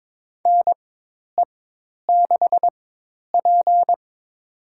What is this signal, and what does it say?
Morse code, heard as a keyed tone, about 22 wpm, spaced as for 10 wpm, 700 Hz. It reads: NE6P
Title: USA Call Signs - 4 Characters - 4x Character Spacing - 22wpm